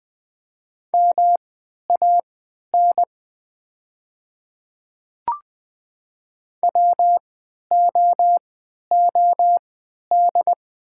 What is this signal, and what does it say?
Morse code, heard as a keyed tone, about 20 wpm, spaced as for 11 wpm, 700 Hz. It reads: MAN WOOD
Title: 3x Character Spacing - Top 300 Words - 20wpm